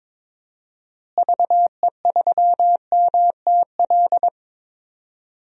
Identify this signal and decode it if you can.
Morse code, heard as a keyed tone, about 22 wpm, 700 Hz. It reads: VE3MTL
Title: DXCC-100 Call Signs - 6 Characters - Repeated 2x - 22wpm